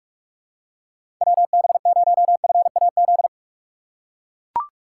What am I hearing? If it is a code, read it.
Morse code, heard as a keyed tone, about 44 wpm, 700 Hz. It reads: WB0FAZ
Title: USA Call Signs - 6 Characters - 8x Character Spacing - 45wpm